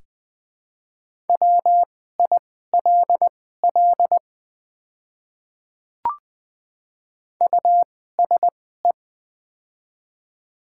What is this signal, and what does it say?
Morse code, heard as a keyed tone, about 20 wpm, spaced as for 14 wpm, 700 Hz. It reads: WILL USE
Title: 2x Character Spacing - Top 100 Words - 20wpm